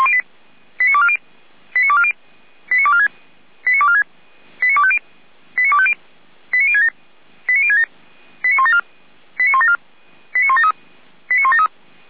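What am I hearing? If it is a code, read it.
Five-tone selective call (ZVEI), heard as quick runs of five tones, tones of about 70 ms. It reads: 89240 89240 89246 89246 89250 89250 89067 89067 89174 89174 89173 89173
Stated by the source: ZVEI five-tone selective call for a fire brigade, received on an SDR.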